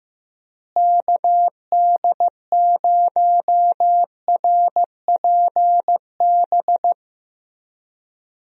Morse code, heard as a keyed tone, about 15 wpm, 700 Hz. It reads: KD0RPB
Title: USA Call Signs - 6 Characters - Repeated 3x - 15wpm